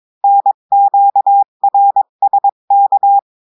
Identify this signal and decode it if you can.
Morse code, heard as a keyed tone, about 22 wpm, 800 Hz. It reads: NQRSK
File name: cours-du-mercredi / audios / cours-853-du-2-mai-2024-etxte chiffre et texte clair-a-20-wpm